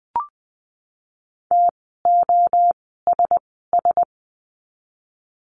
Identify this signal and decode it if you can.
Morse code, heard as a keyed tone, about 20 wpm, 700 Hz. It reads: T O S S
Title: 061 Lesson 11f - Letter h - Words 20wpm